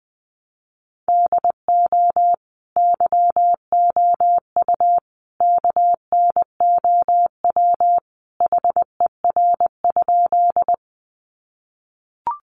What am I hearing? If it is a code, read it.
Morse code, heard as a keyed tone, about 20 wpm, 700 Hz. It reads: DO YOU KNOW HER?
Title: Sentences from Top 100 Words 20wpm